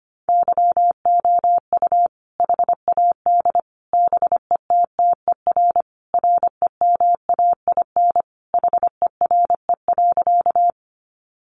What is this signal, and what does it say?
Morse code, heard as a keyed tone, about 25 wpm, 700 Hz. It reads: YOU HAD BETTER REMAIN HERE.
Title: Sentences from Top 1200 Words 25wpm